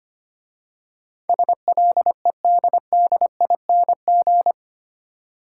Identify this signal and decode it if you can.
Morse code, heard as a keyed tone, about 28 wpm, 700 Hz. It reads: SLEDDING
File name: Easy Words - Speed-Racing with Hints - 25wpm